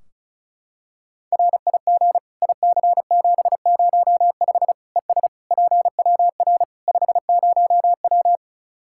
Morse code, heard as a keyed tone, about 34 wpm, 700 Hz. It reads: RIG IC705 ES PWR 50W
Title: Practice QSO - Exchange 2 - Info - 35wpm